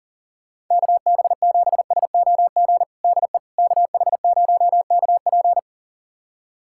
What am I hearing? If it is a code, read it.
Morse code, heard as a keyed tone, about 40 wpm, 700 Hz. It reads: KB7SOG DE KH0KP
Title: USA Callsign DE Callsign 40wpm